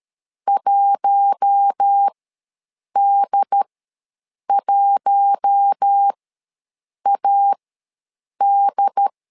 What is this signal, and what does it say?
Morse code, heard as a keyed tone, about 13 wpm, spaced as for 7 wpm, 800 Hz. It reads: 1D1AD
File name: LEZIONI / 4 Q1 ws 12wpm 2129